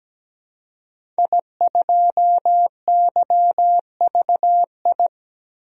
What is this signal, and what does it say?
Morse code, heard as a keyed tone, about 17 wpm, 700 Hz. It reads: I2YVI
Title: DXCC-100 Call Signs - 5 Characters - Repeated 3x - 17wpm